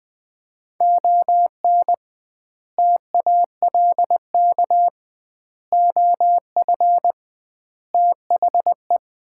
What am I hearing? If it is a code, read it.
Morse code, heard as a keyed tone, about 21 wpm, 700 Hz. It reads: ON TALK OF THE
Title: 2x Word-Spacing - Sets of 5 Words - 20wpm